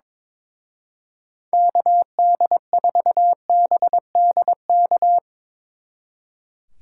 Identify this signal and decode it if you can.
Morse code, heard as a keyed tone, about 25 wpm, 700 Hz. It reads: KD4BDK
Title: Speed Racing - USA 2x3 Call Signs 22wpm